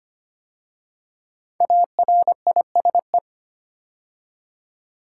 Morse code, heard as a keyed tone, about 25 wpm, 700 Hz. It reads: ARISE
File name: 5 Letter Words - First 3 Letters Given - 25wpm